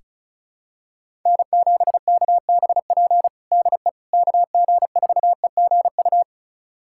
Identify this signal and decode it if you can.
Morse code, heard as a keyed tone, about 34 wpm, 700 Hz. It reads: N7KBP DE KG4EGU
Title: USA Callsign DE Callsign 35wpm